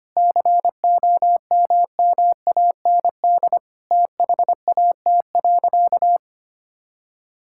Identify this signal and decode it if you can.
Morse code, heard as a keyed tone, about 25 wpm, 700 Hz. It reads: COMMAND THAT.
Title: Sentences from Top 1600 Words 25wpm